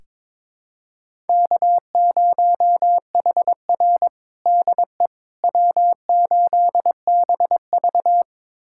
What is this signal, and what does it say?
Morse code, heard as a keyed tone, about 22 wpm, 700 Hz. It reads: K0HR DE W8BV
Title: USA Callsign DE Callsign - 4 Characters - 22wpm